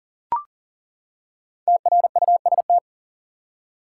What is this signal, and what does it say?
Morse code, heard as a keyed tone, about 39 wpm, 700 Hz. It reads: TRUST
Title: Five Letter Words - 40wpm